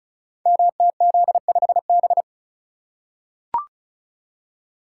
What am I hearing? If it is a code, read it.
Morse code, heard as a keyed tone, about 34 wpm, 700 Hz. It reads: MTZ5B
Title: Code Groups - 5 Letter:Numbers - 35wpm